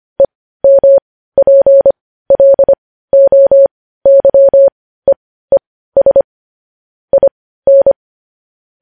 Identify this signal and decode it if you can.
Morse code, heard as a keyed tone, about 25 wpm, spaced as for 15 wpm, 550 Hz. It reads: EMPLOYEES IN
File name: podcast / News headlines-1x-25x15